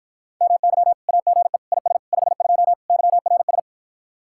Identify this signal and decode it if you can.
Morse code, heard as a keyed tone, about 52 wpm, 700 Hz. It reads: MY AGE IS 52 YRS